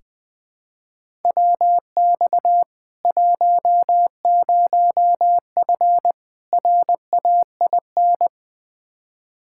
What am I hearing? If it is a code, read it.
Morse code, heard as a keyed tone, about 20 wpm, 700 Hz. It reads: WX 10F RAIN
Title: K1USN SST QSO Practice 20wpm